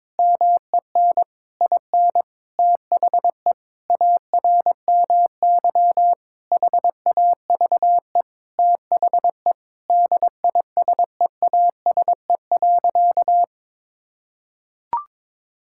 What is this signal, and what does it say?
Morse code, heard as a keyed tone, about 22 wpm, 700 Hz. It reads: MEN IN THE ARMY HAVE THE DISEASE.
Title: Sentences from Top 1000 Words - Review Part 6 - 22wpm